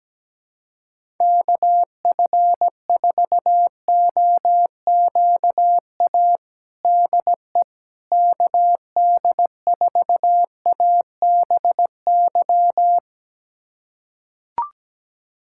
Morse code, heard as a keyed tone, about 17 wpm, 700 Hz. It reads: KF4OQA DE KD4ABY
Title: USA Callsign DE Callsign 17wpm